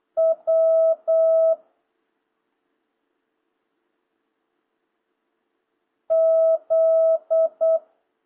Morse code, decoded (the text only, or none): W Z